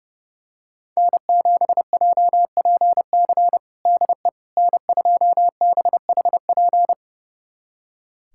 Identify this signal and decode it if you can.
Morse code, heard as a keyed tone, about 32 wpm, 700 Hz. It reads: N7JPC DE N2BHP
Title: USA Callsign DE Callsign - 5 Characters - 30wpm